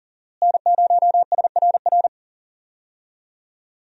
Morse code, heard as a keyed tone, about 40 wpm, 700 Hz. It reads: N0SRR
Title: USA Call Signs - 5 Characters - Repeated 2x - 40wpm